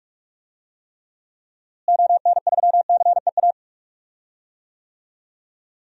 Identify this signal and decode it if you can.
Morse code, heard as a keyed tone, about 45 wpm, 700 Hz. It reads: ON3CEA
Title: DXCC-100 Call Signs - 6 Characters - Repeated 3x - 45wpm